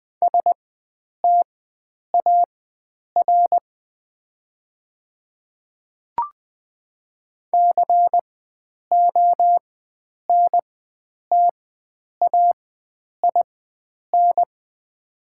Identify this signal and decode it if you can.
Morse code, heard as a keyed tone, about 20 wpm, spaced as for 9 wpm, 700 Hz. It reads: STAR CONTAIN
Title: 4x Character Spacing - Top 500 Words - 20wpm